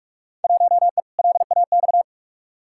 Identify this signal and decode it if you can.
Morse code, heard as a keyed tone, about 43 wpm, 700 Hz. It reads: 1E LAX
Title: Field Day Exchanges - 45wpm